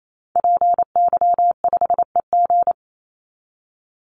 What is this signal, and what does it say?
Morse code, heard as a keyed tone, about 28 wpm, 700 Hz. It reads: PY5EG
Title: Call Signs - 28wpm